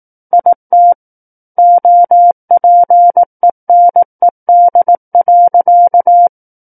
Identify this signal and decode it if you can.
Morse code, heard as a keyed tone, about 18 wpm, 700 Hz. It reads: IT OPENED.
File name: 18 WPM